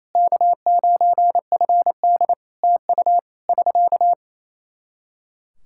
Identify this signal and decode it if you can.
Morse code, heard as a keyed tone, about 28 wpm, 700 Hz. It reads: K9FD TU <SK>
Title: Practice QSO - Exchange 4 - Ending - 28wpm